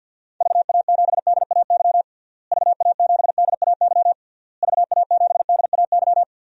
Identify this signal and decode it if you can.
Morse code, heard as a keyed tone, about 49 wpm, 700 Hz. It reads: VA7DAY VA7DAY VA7DAY
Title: DXCC-100 Call Signs - 6 Characters - Repeated 3x - 50wpm